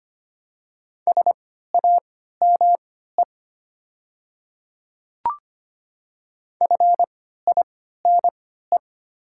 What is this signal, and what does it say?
Morse code, heard as a keyed tone, about 25 wpm, spaced as for 14 wpm, 700 Hz. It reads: SAME FINE
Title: 3x Character Spacing - Top 500 Words - 25wpm